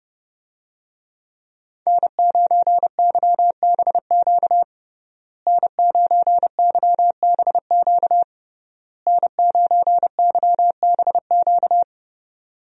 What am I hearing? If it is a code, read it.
Morse code, heard as a keyed tone, about 30 wpm, 700 Hz. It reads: N9YBQ N9YBQ N9YBQ
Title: USA Call Signs - 5 Characters - Repeated 3x - 30wpm